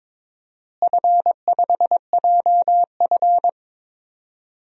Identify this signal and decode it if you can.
Morse code, heard as a keyed tone, about 22 wpm, 700 Hz. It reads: F5JF
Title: DXCC-100 Call Signs - 4 Characters - Repeated 3x - 22wpm